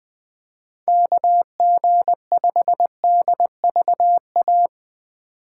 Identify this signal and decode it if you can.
Morse code, heard as a keyed tone, about 20 wpm, 700 Hz. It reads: KG5DVA